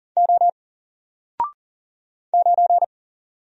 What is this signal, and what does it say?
Morse code, heard as a keyed tone, about 40 wpm, 700 Hz. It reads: O 9
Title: Single Letter-Number 40wpm